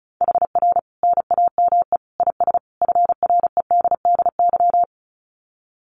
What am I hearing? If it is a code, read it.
Morse code, heard as a keyed tone, about 34 wpm, 700 Hz. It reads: HR NAME IS FREDDY
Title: Common QSO Phrases 35wpm